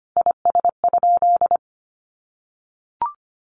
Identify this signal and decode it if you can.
Morse code, heard as a keyed tone, about 25 wpm, 700 Hz. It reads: IS?
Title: Sentences from Top 100 Words 25wpm